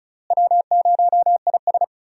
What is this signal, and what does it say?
Morse code, heard as a keyed tone, about 36 wpm, 700 Hz. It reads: W0IS
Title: USA Call Signs - 4 Characters - Repeated 3x - 35wpm